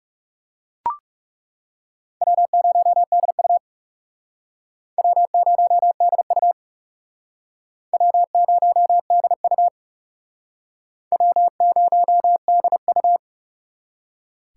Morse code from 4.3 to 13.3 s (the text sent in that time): W0DU W0DU W0DU